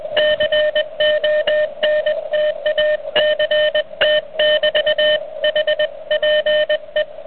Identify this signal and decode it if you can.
Morse code, heard as a keyed tone, about 20 wpm, 610 Hz, with noise. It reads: CONTACT <BT> HPE
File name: truSDX / IK3OTW